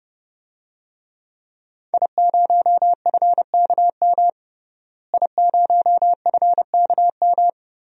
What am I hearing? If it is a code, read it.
Morse code, heard as a keyed tone, about 32 wpm, 700 Hz. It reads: I0FKM I0FKM
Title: DXCC-100 Call Signs - 5 Characters - Repeated 3x - 30wpm